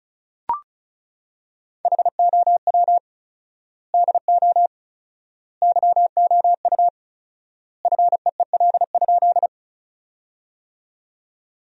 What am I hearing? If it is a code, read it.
Morse code, heard as a keyed tone, about 34 wpm, 700 Hz. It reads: HOW DO YOU FEEL?
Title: 4x Word-Spacing - Sentences from Top 400 Words - 35wpm